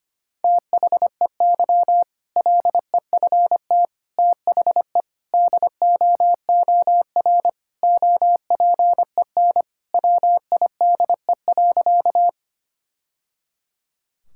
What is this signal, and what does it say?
Morse code, heard as a keyed tone, about 25 wpm, 700 Hz. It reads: THEY LEFT THE DOOR OPEN WIDE.
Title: Sentences from Top 1000 Words - Review Part 6 - 25wpm